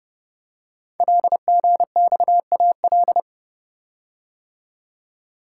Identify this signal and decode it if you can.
Morse code, heard as a keyed tone, about 29 wpm, 690 Hz. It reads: LGXAL
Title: Code Groups - 5 Letters - 30wpm